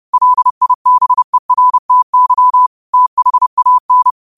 Morse code, heard as a keyed tone, about 30 wpm, 1000 Hz. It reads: LIBERTY THAN
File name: qotdfiles / 30wpm